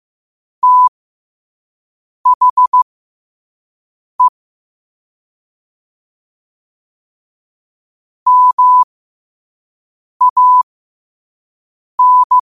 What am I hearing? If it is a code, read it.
Morse code, heard as a keyed tone, about 15 wpm, spaced as for 5 wpm, 1000 Hz. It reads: THE MAN
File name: qotdfiles / 5wpm